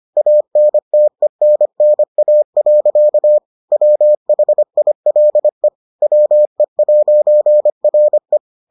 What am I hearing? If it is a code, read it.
Morse code, heard as a keyed tone, about 25 wpm, 600 Hz. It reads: ANTENNA. WHILE WE'RE